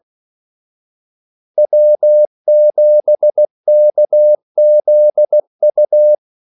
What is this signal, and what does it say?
Morse code, heard as a keyed tone, about 16 wpm, 600 Hz. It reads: W7KZU
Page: Kaldesignaler 16 wpm